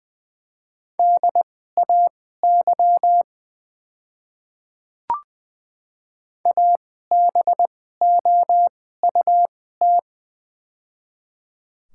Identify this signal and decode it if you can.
Morse code, heard as a keyed tone, about 20 wpm, spaced as for 15 wpm, 690 Hz. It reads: DAY ABOUT
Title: 2x Character Spacing - Top 500 Words - 20wpm